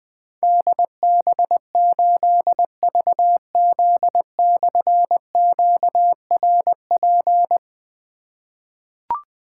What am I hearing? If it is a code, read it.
Morse code, heard as a keyed tone, about 20 wpm, 700 Hz. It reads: DB8VZ/QRP
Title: Difficult Call Signs 20wpm